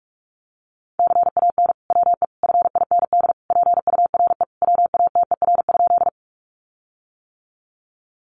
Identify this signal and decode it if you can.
Morse code, heard as a keyed tone, about 44 wpm, 700 Hz. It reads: CAN WE FIND PURE WATER?
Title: Sentences from Top 1500 Words 45wpm